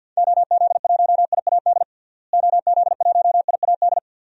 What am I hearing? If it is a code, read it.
Morse code, heard as a keyed tone, about 50 wpm, 700 Hz. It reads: OZ1IAD OZ1IAD
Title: DXCC-100 Call Signs - 6 Characters - Repeated 2x - 50wpm